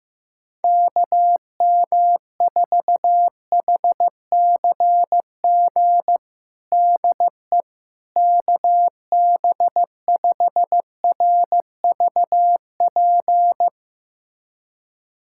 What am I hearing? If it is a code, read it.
Morse code, heard as a keyed tone, about 15 wpm, 700 Hz. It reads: KM4HCG DE KB5RVP
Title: USA Callsign DE Callsign 15wpm